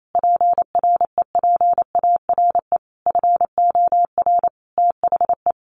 Morse code, heard as a keyed tone, about 28 wpm, 700 Hz. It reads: PREPARE FOR THE
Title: Sentences from Top 1200 Words 28wpm